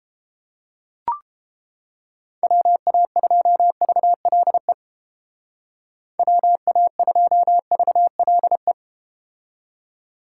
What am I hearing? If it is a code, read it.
Morse code, heard as a keyed tone, about 31 wpm, 700 Hz. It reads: WA2VLE WA2VLE